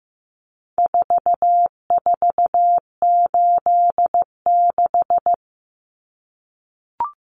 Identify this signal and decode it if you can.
Morse code, heard as a keyed tone, about 15 wpm, 700 Hz. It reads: 4486
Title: Numbers - 4-Digit 15wpm